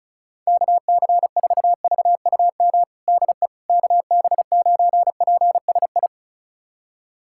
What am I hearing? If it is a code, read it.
Morse code, heard as a keyed tone, about 36 wpm, 700 Hz. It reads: KC4VUM DE KB9PSI
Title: USA Callsign DE Callsign - 6 Characters - 35wpm